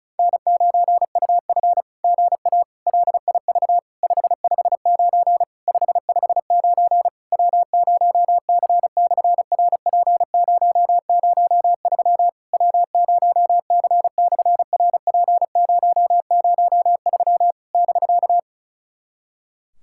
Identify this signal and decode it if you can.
Morse code, heard as a keyed tone, about 36 wpm, 690 Hz. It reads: N9UF GA LIV 559 559 W0C/RP003 W0C/RP003 <BK>
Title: SOTA QSO Practice 35wpm